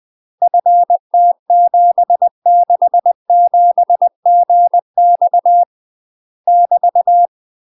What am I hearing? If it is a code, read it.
Morse code, heard as a keyed tone, about 20 wpm, 700 Hz. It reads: FT767GX <BT>
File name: UK MASTER QSOs TRAINING/20WPM